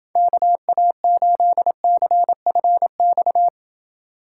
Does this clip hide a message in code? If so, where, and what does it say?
Yes: Morse code from 0.2 to 3.5 s, heard as a keyed tone, about 30 wpm, 700 Hz. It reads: KA8CFX